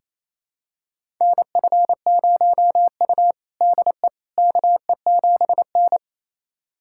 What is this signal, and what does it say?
Morse code, heard as a keyed tone, about 28 wpm, 700 Hz. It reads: NF0U DE KE7N
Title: USA Callsign DE Callsign - 4 Characters - 28wpm